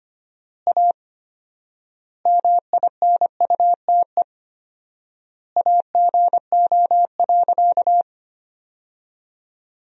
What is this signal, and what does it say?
Morse code, heard as a keyed tone, about 25 wpm, 700 Hz. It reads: A MINUTE AGO.
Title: 4x Word-Spacing - Sentences from Top 500 Words - 25wpm